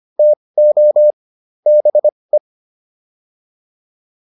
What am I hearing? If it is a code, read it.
Morse code, heard as a keyed tone, about 25 wpm, spaced as for 20 wpm, 600 Hz. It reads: TO BE
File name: WD405 20